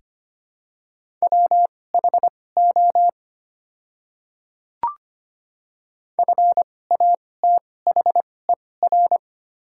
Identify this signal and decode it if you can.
Morse code, heard as a keyed tone, about 25 wpm, spaced as for 18 wpm, 700 Hz. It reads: WHO FATHER
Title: 2x Character Spacing - Top 300 Words - 25wpm